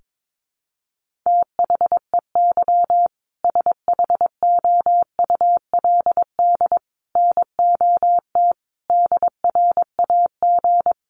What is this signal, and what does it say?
Morse code, heard as a keyed tone, about 22 wpm, 700 Hz. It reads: THEY SHOULD NOT DRAG
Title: Sentences from Top 1500 Words 22wpm